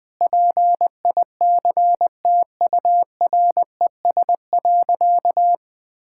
Morse code, heard as a keyed tone, about 20 wpm, 700 Hz. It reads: PICTURES.
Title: Sentences with 8-Word Phrases - 20wpm